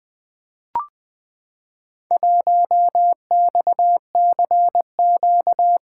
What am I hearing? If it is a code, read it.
Morse code, heard as a keyed tone, about 20 wpm, 700 Hz. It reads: 1XCQ
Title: Four Letter-Number Combinations - 20wpm